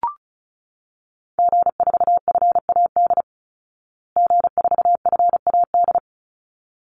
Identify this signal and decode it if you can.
Morse code, heard as a keyed tone, about 34 wpm, 700 Hz. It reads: G4FAD G4FAD
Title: Call Signs Repeated 3x - 35wpm